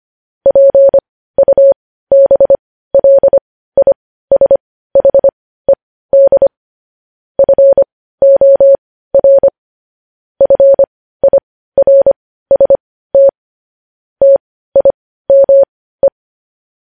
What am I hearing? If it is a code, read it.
Morse code, heard as a keyed tone, about 25 wpm, spaced as for 15 wpm, 550 Hz. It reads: PUBLISHED FOR FIRST TIME
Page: news headlines in morse code at 15 wpm